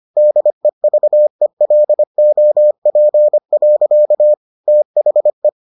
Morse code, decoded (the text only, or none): DEVELOP. THE